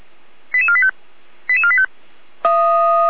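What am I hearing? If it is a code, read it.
Five-tone selective call (ZVEI), heard as quick runs of five tones, tones of about 70 ms. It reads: 88485 88485
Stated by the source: ZVEI five-tone selective call received on an SDR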